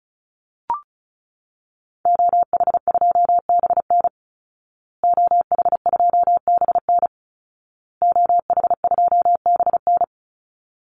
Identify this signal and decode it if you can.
Morse code, heard as a keyed tone, about 34 wpm, 700 Hz. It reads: OH2BN OH2BN OH2BN
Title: Call Signs Repeated 2x - 35wpm